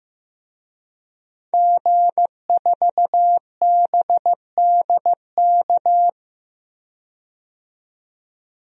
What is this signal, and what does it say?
Morse code, heard as a keyed tone, about 16 wpm, 700 Hz. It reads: G4BDK
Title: DXCC-100 Call Signs - 5 Characters - 15wpm